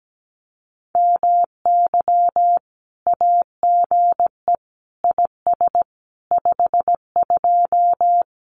Morse code, heard as a keyed tone, about 17 wpm, 700 Hz. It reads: MY AGE IS 52
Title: Common QSO Phrases Encore 17wpm